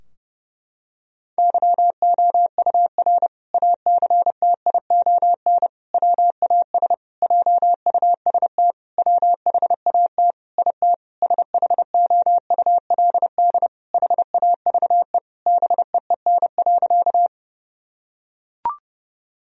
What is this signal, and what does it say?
Morse code, heard as a keyed tone, about 30 wpm, 700 Hz. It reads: YOUR ACTION WAS JUST WHAT IT SHOULD HAVE BEEN.
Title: Sentences from Top 1000 Words - Review Part 4 - 30wpm